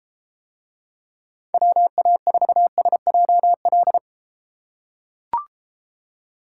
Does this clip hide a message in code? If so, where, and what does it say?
Yes: Morse code from 1.5 to 4.0 s, heard as a keyed tone, about 30 wpm, 700 Hz. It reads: WA4SJL